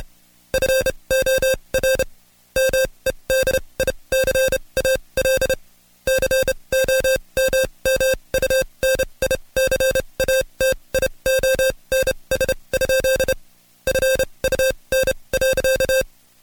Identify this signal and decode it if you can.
Morse code, heard as a keyed tone, about 29 wpm, spaced as for 23 wpm, 530 Hz. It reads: FOR MEDICAL COMMUNICATIONS? FUN.
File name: Code Quick / AudioDownld2 / Seg138-Stepper11 23WPM